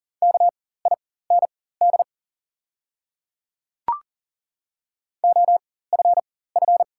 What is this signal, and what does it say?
Morse code, heard as a keyed tone, about 40 wpm, spaced as for 19 wpm, 700 Hz. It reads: KIND OFF
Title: 4x Character Spacing - Top 300 Words - 40wpm